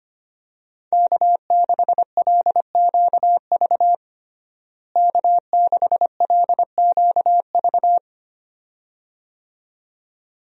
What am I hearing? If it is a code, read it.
Morse code, heard as a keyed tone, about 25 wpm, 700 Hz. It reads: K6LQV K6LQV
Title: USA Call Signs - 5 Characters - Repeated 2x - 25wpm